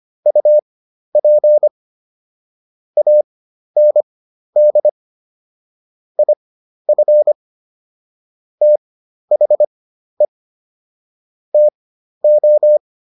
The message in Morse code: UP AND IF THE TO

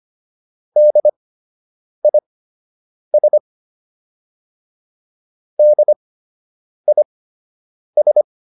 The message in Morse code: DIS DIS

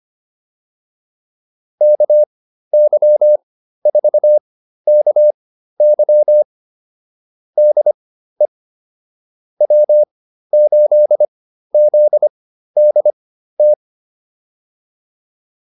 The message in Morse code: KY4KY DE W8ZDT

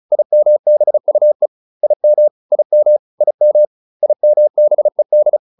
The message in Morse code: IMBUE IM IM IM IMBED

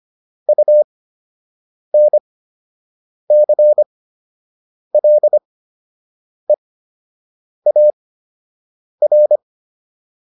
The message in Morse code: UNCLEAR

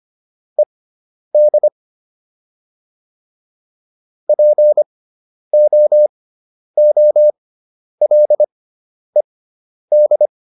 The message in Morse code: ED POOLED

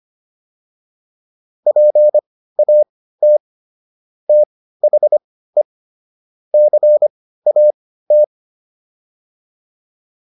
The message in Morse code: PAT THE CAT

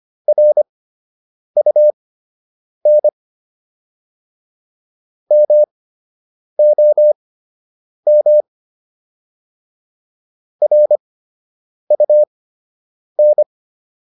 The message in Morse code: RUN MOM RUN

